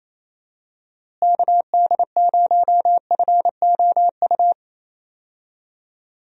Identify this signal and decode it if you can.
Morse code, heard as a keyed tone, about 28 wpm, 700 Hz. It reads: KD0FOU
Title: USA Call Signs - 6 Characters - Repeated 2x - 28wpm